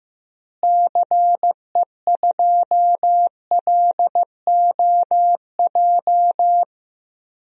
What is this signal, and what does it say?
Morse code, heard as a keyed tone, about 15 wpm, 700 Hz. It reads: CE2LOJ
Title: DXCC-100 Call Signs - 6 Characters - Repeated 3x - 15wpm